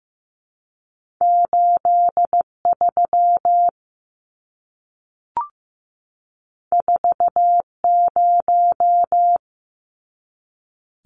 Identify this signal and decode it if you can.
Morse code, heard as a keyed tone, about 15 wpm, 700 Hz. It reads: 8 3 4 0
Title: Numbers - 2-Digit 15wpm